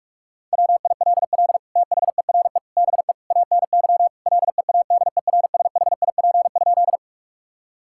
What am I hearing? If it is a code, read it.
Morse code, heard as a keyed tone, about 41 wpm, 700 Hz. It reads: WILL THERE BE ANY LEADERSHIP?
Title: Sentences from Top 1600 Words 45wpm